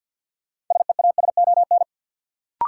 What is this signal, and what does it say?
Morse code, heard as a keyed tone, about 52 wpm, 700 Hz. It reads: SEASON
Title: Sets of 5 Words - New Word Drill - 50wpm